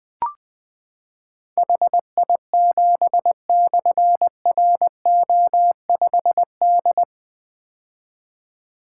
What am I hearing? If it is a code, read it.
Morse code, heard as a keyed tone, about 20 wpm, 700 Hz. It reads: HI7/RO5D
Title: Difficult Call Signs 20wpm